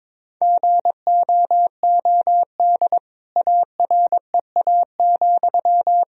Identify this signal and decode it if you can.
Morse code, heard as a keyed tone, about 22 wpm, 700 Hz. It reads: GOOD AREA,
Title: Sentences from Top 1000 Words - Review Part 2 - 22wpm